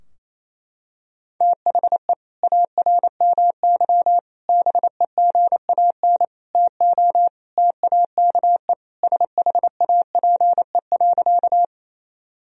Morse code, heard as a keyed tone, about 28 wpm, 700 Hz. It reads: THE ARMY BEGAN TO TAKE SHAPE.
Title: Sentences from Top 1000 Words - Review Part 3 - 28wpm